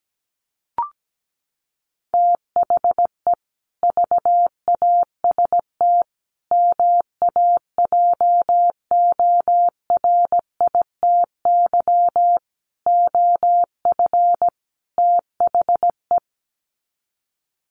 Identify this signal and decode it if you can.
Morse code, heard as a keyed tone, about 17 wpm, 700 Hz. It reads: THE VAST MAJORITY OF THE
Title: Sets of 5 Words Encore 17wpm